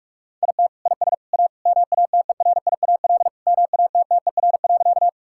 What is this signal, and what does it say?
Morse code, heard as a keyed tone, about 48 wpm, 700 Hz. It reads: IT IS A MATERIAL MATTER.
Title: Sentences from Top 1000 Words - Review Part 5 - 45wpm